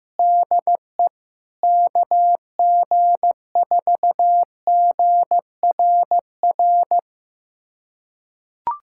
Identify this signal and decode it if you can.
Morse code, heard as a keyed tone, about 16 wpm, 700 Hz. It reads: DE KG4GRR
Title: USA Callsign DE Callsign - 6 Characters - 15wpm